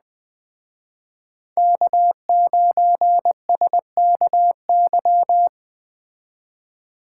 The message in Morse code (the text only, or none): K9SKY